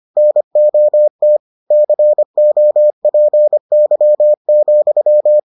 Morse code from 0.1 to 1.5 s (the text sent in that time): NOT